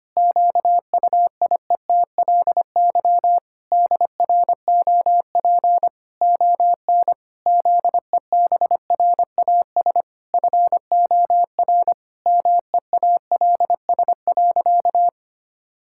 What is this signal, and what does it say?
Morse code, heard as a keyed tone, about 25 wpm, 700 Hz. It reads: QUIETLY DROP ON ZEBRAS FOR MEALS.
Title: Pangram Challenge v2 - 25wpm